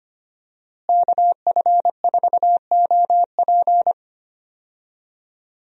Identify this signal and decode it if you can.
Morse code, heard as a keyed tone, about 25 wpm, 700 Hz. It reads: KF4OP
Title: USA Call Signs - 5 Characters - Repeated 3x - 25wpm